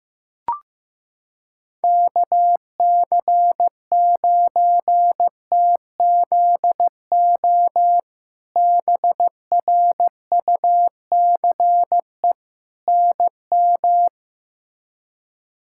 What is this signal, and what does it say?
Morse code, heard as a keyed tone, about 15 wpm, 700 Hz. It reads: KC9TZO BRUCE NM
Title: K1USN SST QSO Practice - 15wpm